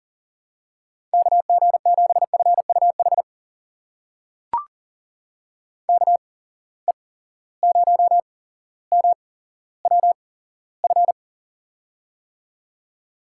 Morse code, heard as a keyed tone, about 42 wpm, spaced as for 17 wpm, 700 Hz. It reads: KG7FUH KE0MWF
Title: USA Call Signs - 6 Characters - 8x Character Spacing - 40wpm